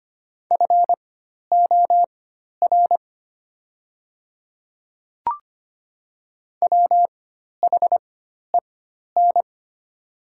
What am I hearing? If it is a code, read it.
Morse code, heard as a keyed tone, about 25 wpm, spaced as for 12 wpm, 690 Hz. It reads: FOR WHEN
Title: 4x Character Spacing - Top 300 Words - 25wpm